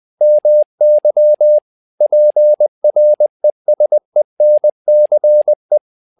Morse code, heard as a keyed tone, about 20 wpm, 600 Hz. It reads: MY PRESENCE
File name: learning center / tower trouble / tower trouble 20wpm